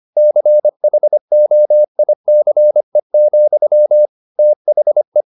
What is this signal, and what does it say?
Morse code, heard as a keyed tone, about 25 wpm, 600 Hz. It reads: CHOICE, THE
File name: the old man and the sea part 1 25wpm